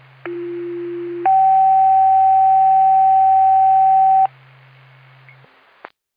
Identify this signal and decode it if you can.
Two-tone page, heard as a one-second tone, then a three-second tone, about 350 then 770 Hz. It is heard from 0.3 to 4.3 s.